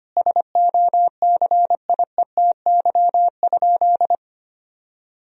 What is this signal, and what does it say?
Morse code, heard as a keyed tone, about 25 wpm, 700 Hz. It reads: SOCIETY?
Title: Sentences from Top 1000 Words - Review Part 4 - 25wpm